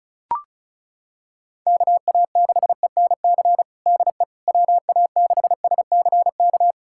Morse code, heard as a keyed tone, about 36 wpm, 700 Hz. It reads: KA6ENC DE WA6SCK
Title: USA Callsign DE Callsign - 6 Characters - 35wpm